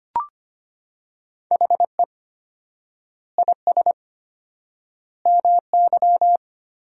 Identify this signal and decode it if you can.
Morse code, heard as a keyed tone, about 25 wpm, 700 Hz. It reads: HE IS MY
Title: 4x Word-Spacing - Sentences from Top 300 Words - 25wpm